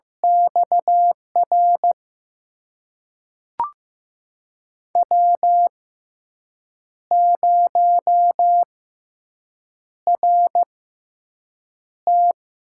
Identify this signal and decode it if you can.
Morse code, heard as a keyed tone, about 15 wpm, spaced as for 6 wpm, 700 Hz. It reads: XR W0RT
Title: USA Call Signs - 6x Character Spacing - 15wpm